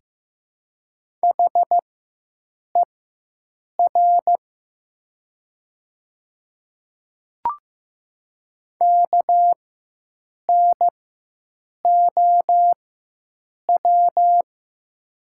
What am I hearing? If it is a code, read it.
Morse code, heard as a keyed tone, about 15 wpm, spaced as for 7 wpm, 700 Hz. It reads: HER KNOW
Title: 4x Character Spacing - Top 300 Words - 15wpm